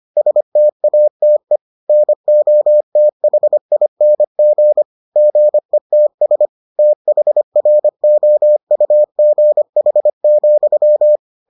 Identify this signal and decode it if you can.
Morse code, heard as a keyed tone, about 25 wpm, 600 Hz. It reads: STATE NOTHING GETS THROUGH,